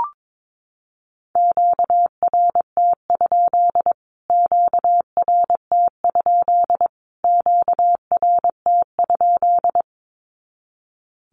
Morse code, heard as a keyed tone, about 22 wpm, 700 Hz. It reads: QRT? QRT? QRT?
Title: Q-Codes Repeated 3x 22wpm